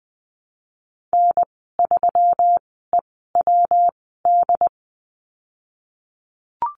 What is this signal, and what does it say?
Morse code, heard as a keyed tone, about 20 wpm, spaced as for 15 wpm, 700 Hz. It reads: N3EWD
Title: 122 Lesson 20g - Number 3 - Call Signs 20wpm